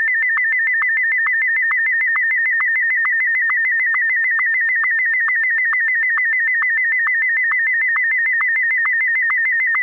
SSTV (slow-scan television), Martin 1.